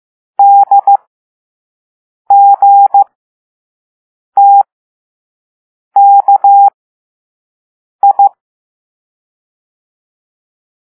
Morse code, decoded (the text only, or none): DGTKI